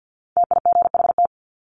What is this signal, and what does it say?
Morse code, heard as a keyed tone, about 50 wpm, 700 Hz. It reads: TIGHT
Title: Top 1751-2000 Words 50wpm